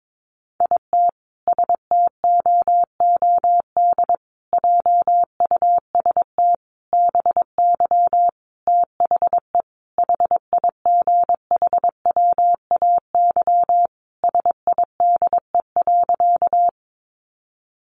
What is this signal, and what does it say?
Morse code, heard as a keyed tone, about 22 wpm, 700 Hz. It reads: IT STOOD JUST BY THE HIGHWAY SIDE.
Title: Sentences from Top 1400 Words 22wpm